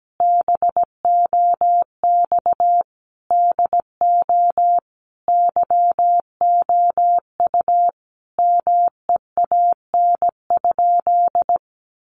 Morse code, heard as a keyed tone, about 17 wpm, 700 Hz. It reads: BOX DO YOU MEAN?
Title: Sentences from Top 500 Words 17wpm